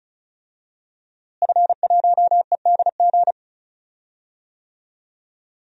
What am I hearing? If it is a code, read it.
Morse code, heard as a keyed tone, about 36 wpm, 700 Hz. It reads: F1EDG
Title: DXCC-100 Call Signs - 5 Characters - Repeated 2x - 35wpm